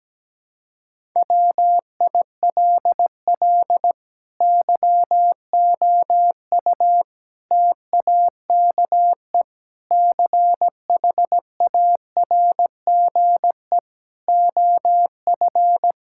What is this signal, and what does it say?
Morse code, heard as a keyed tone, about 17 wpm, 700 Hz. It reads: WILL YOU TAKE CHARGE OF
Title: Sentences from Top 1000 Words - Review Part 6 - 17wpm